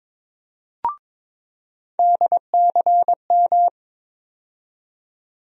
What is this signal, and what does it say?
Morse code, heard as a keyed tone, about 22 wpm, 700 Hz. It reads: DCM
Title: USA Call Sign Postfixes - 3 Letters 22wpm